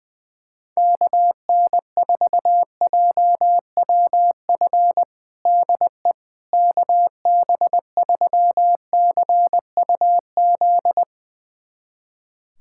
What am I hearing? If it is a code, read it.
Morse code, heard as a keyed tone, about 20 wpm, 700 Hz. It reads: KN4JWF DE KB3CUZ
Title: USA Callsign DE Callsign 20wpm